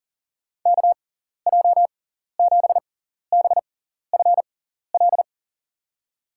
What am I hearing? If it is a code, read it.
Morse code, heard as a keyed tone, about 37 wpm, spaced as for 14 wpm, 700 Hz. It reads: KJ7BFL
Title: USA Call Signs - 6 Characters - 6x Character Spacing - 40wpm